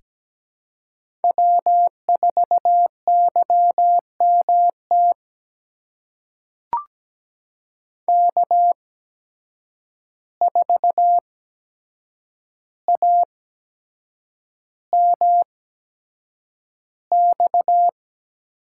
Morse code, heard as a keyed tone, about 17 wpm, spaced as for 7 wpm, 700 Hz. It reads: W4YMT K4AMX
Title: USA Call Signs - 5 Characters - 8x Character Spacing - 17wpm